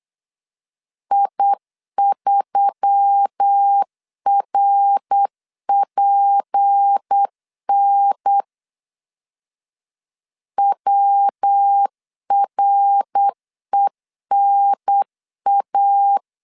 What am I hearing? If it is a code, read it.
Morse code, heard as a keyed tone, about 8 wpm, 800 Hz. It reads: I3RPN WRENA
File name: LEZIONI / 6 PKX0 / 6 PK 8wpm 3003